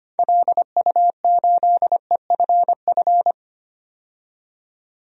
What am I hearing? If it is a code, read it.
Morse code, heard as a keyed tone, about 25 wpm, 700 Hz. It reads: LU8EFF
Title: DXCC-100 Call Signs - 6 Characters - 25wpm